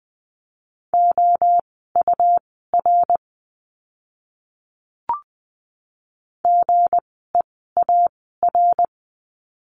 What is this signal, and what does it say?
Morse code, heard as a keyed tone, about 20 wpm, spaced as for 14 wpm, 700 Hz. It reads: OUR GEAR
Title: 160 Lesson 26f - Letter g - Words 20wpm